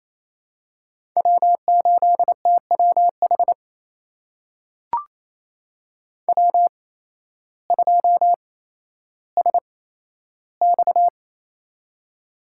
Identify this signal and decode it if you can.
Morse code, heard as a keyed tone, about 28 wpm, spaced as for 13 wpm, 700 Hz. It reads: W8TWH W2SX
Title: USA Call Signs - 8x Character Spacing - 28wpm